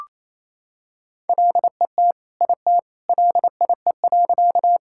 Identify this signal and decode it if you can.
Morse code, heard as a keyed tone, about 28 wpm, 700 Hz. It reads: LET IT LIE.
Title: Sentences from Top 1600 Words 28wpm